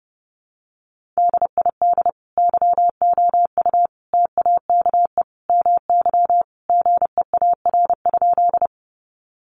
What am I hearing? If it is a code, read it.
Morse code, heard as a keyed tone, about 30 wpm, 700 Hz. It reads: DID YOU TAKE MY GEAR?
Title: Sentences from Top 1400 Words 30wpm